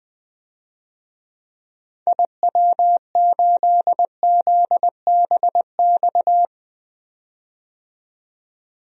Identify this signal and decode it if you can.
Morse code, heard as a keyed tone, about 20 wpm, 700 Hz. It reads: IW8ZBX
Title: DXCC-100 Call Signs - 6 Characters - 20wpm